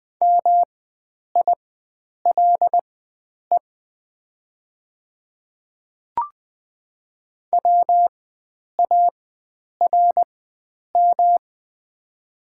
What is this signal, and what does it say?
Morse code, heard as a keyed tone, about 20 wpm, spaced as for 9 wpm, 700 Hz. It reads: MILE WARM
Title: 4x Character Spacing - Top 500 Words - 20wpm